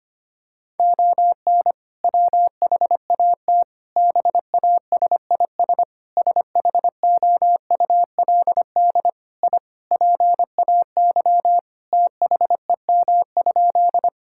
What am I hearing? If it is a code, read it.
Morse code, heard as a keyed tone, about 25 wpm, 700 Hz. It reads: ON WHAT BASIS SHOULD I PAY THEM?
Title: Sentences from Top 1000 Words - Review Part 5 - 25wpm